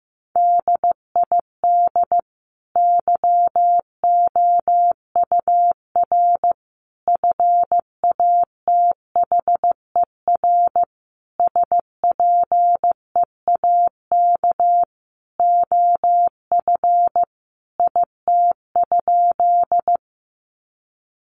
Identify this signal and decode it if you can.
Morse code, heard as a keyed tone, about 15 wpm, 700 Hz. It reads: DID YOUR FATHER SPEAK OF IT?
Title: Sentences from Top 1100 Words 15wpm